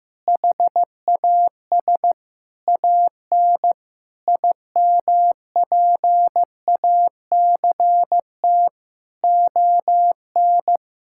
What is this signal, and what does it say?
Morse code, heard as a keyed tone, about 15 wpm, 700 Hz. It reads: HAS AN IMPACT ON
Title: Sentences from Top 1000 Words - Review Part 6 - 15wpm